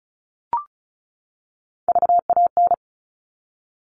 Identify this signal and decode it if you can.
Morse code, heard as a keyed tone, about 34 wpm, 700 Hz. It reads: VAN
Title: All Cut Numbers - 3-Digit 35wpm